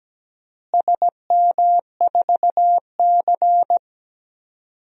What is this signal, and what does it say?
Morse code, heard as a keyed tone, about 17 wpm, 700 Hz. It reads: SM4C
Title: DXCC-100 Call Signs - 4 Characters - Repeated 2x - 17wpm